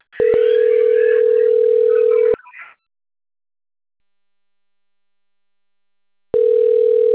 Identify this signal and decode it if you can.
Ringback tone (North America), 440 + 480 Hz.